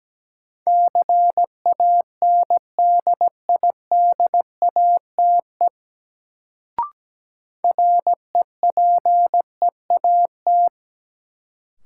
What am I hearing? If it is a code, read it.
Morse code, heard as a keyed tone, about 17 wpm, 700 Hz. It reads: CANDIDATE REPEAT
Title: Sentences from Top 1600 Words - New Word Drill Alt - 17wpm